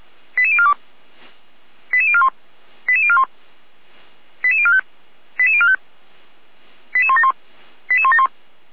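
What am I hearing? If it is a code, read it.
Five-tone selective call (ZVEI), heard as quick runs of five tones, tones of about 70 ms. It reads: 88042 88041 88041 88045 88045 88171 88171
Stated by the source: ZVEI five-tone selective call received by SDR.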